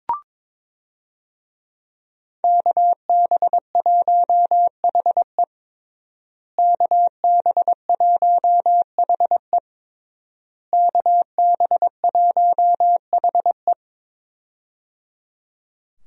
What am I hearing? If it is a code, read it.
Morse code, heard as a keyed tone, about 22 wpm, 700 Hz. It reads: KB1HE KB1HE KB1HE
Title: USA Call Signs - 5 Characters - Repeated 3x - 22wpm